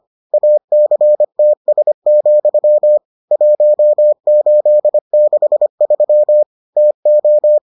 Morse code, heard as a keyed tone, about 25 wpm, 600 Hz. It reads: ACTS, 1863 TO